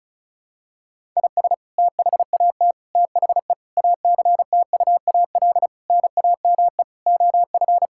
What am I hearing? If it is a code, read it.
Morse code, heard as a keyed tone, about 36 wpm, 700 Hz. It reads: IS THAT THE ACTUAL NAME OF
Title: Sentences from Top 1000 Words - Review Part 5 - 35wpm